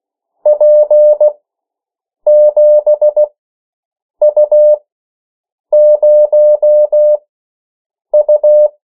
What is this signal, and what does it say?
Morse code, heard as a keyed tone, about 16 wpm, spaced as for 7 wpm, 600 Hz. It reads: P7U0U
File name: morse mp3 / 14 urok 13